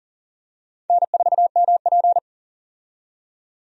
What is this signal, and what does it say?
Morse code, heard as a keyed tone, about 40 wpm, 700 Hz. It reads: N4MP
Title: USA Call Signs - 4 Characters - 40wpm